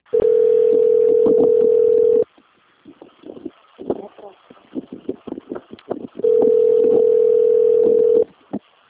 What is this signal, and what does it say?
Ringback tone (North America), 440 + 480 Hz, with speech.
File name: out- - - - - .14311